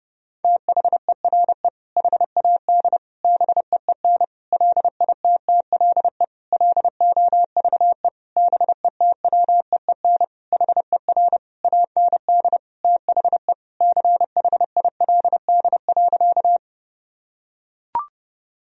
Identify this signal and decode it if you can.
Morse code, heard as a keyed tone, about 32 wpm, 700 Hz. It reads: THERE HAD BEEN LITTLE LOVE BETWEEN HER AND THE CHILD.
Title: Sentences from Top 1000 Words - Review Part 1 - 30wpm